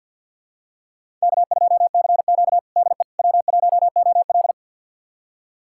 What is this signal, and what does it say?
Morse code, heard as a keyed tone, about 52 wpm, 700 Hz. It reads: K2CQ DE W1OL
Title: USA Callsign DE Callsign - 4 Characters - 50wpm